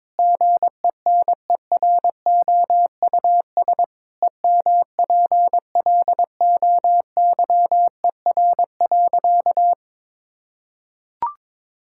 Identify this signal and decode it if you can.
Morse code, heard as a keyed tone, about 21 wpm, 700 Hz. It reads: GENEROUS EMPLOYER.
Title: Pangram Challenge v2 - 22wpm